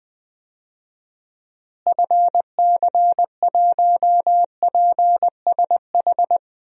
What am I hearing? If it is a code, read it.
Morse code, heard as a keyed tone, about 21 wpm, 700 Hz. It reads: FC1PSH
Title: DXCC-100 Call Signs - 6 Characters - Repeated 2x - 20wpm